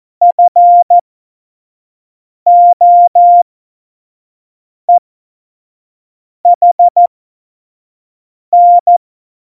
Morse code, heard as a keyed tone, about 14 wpm, spaced as for 5 wpm, 700 Hz. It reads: FOEHN
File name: Morsekurs / Witze / Witz 07 April 2020, 25 F70